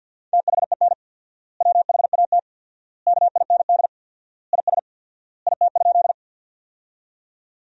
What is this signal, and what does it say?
Morse code, heard as a keyed tone, about 50 wpm, 700 Hz. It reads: THEN WHAT KIND IS IT?
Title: 4x Word-Spacing - Sentences from Top 200 Words - 50wpm